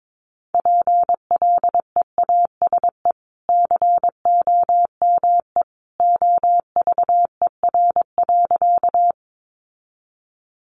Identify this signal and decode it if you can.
Morse code, heard as a keyed tone, about 22 wpm, 700 Hz. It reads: PLEASE COME OVER.
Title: Sentences from Top 1500 Words 22wpm